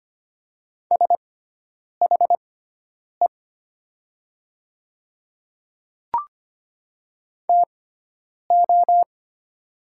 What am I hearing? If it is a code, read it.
Morse code, heard as a keyed tone, about 25 wpm, spaced as for 9 wpm, 700 Hz. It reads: SHE TO
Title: 6x Character Spacing - Top 100 Words - 25wpm